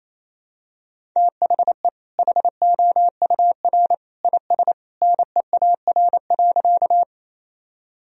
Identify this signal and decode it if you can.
Morse code, heard as a keyed tone, about 28 wpm, 700 Hz. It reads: THE HOUR IS NEAR.
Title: Sentences from Top 1000 Words - Review Part 1 - 28wpm